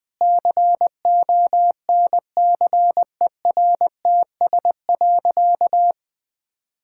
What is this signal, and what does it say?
Morse code, heard as a keyed tone, about 20 wpm, 700 Hz. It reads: CONCERTS.
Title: Sentences with 8-Word Phrases - 20wpm